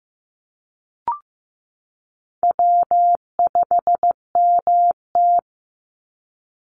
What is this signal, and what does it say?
Morse code, heard as a keyed tone, about 15 wpm, 700 Hz. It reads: W5MT
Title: Easy Call Signs Repeated 3x - 15wpm